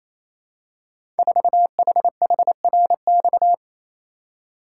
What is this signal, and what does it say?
Morse code, heard as a keyed tone, about 27 wpm, 690 Hz. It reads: 4HHRX
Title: Code Groups - 5 Letter:Numbers - 28wpm